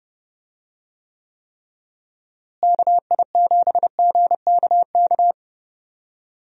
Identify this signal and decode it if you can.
Morse code, heard as a keyed tone, about 30 wpm, 700 Hz. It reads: KI7GKK
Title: USA Call Signs - 6 Characters - Repeated 3x - 30wpm